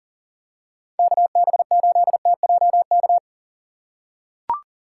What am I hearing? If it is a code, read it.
Morse code, heard as a keyed tone, about 39 wpm, 700 Hz. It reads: KB8TJK
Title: USA Call Signs - 6 Characters - 3x Character Spacing - 40wpm